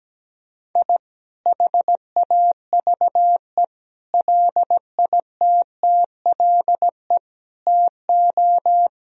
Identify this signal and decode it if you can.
Morse code, heard as a keyed tone, about 17 wpm, 700 Hz. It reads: I HAVE LITTLE TO
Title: Sentences from Top 1000 Words - Review Part 1 - 17wpm